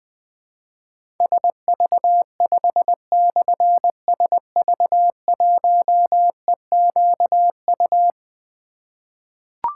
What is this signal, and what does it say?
Morse code, heard as a keyed tone, about 21 wpm, 700 Hz. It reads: SV5/SV1EQU
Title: Difficult Call Signs 20wpm